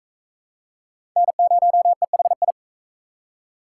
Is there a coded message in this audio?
Yes: Morse code from 1.2 to 2.5 s, heard as a keyed tone, about 41 wpm, 700 Hz. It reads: N0EHI